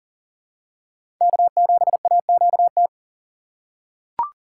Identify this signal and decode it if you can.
Morse code, heard as a keyed tone, about 40 wpm, 700 Hz. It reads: K7AQT
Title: USA Call Signs - 5 Characters - 4x Character Spacing - 40wpm